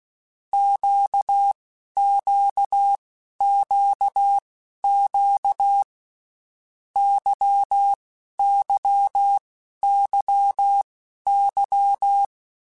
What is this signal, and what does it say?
Morse code, heard as a keyed tone, about 16 wpm, spaced as for 11 wpm, 800 Hz. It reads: QQQQ YYYY
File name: Groups@10 / Groups@10 mp3s / LFCYQ@10